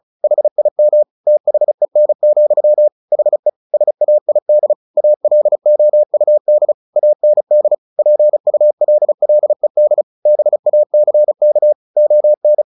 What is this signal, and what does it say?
Morse code, heard as a keyed tone, about 34 wpm, 600 Hz. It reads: HIM THEN, HE SAID ALOUD AND PULLED BACK ON
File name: the old man and the sea / the old man and the sea 1 35wpm